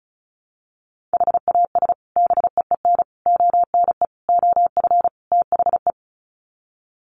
Morse code, heard as a keyed tone, about 34 wpm, 700 Hz. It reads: HAS BEEN ONE OF THE
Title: Sets of 5 Words Encore 35wpm